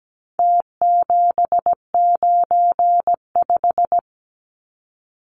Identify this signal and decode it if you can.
Morse code, heard as a keyed tone, about 17 wpm, 700 Hz. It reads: T795
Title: Numbers - Contest Style - 4-Digit 17wpm